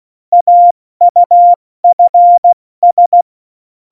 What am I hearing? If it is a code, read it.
Morse code, heard as a keyed tone, about 16 wpm, 700 Hz. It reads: AUFS
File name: Morsekurs / Witze / Witz 10. Oktober 17, Tempo 70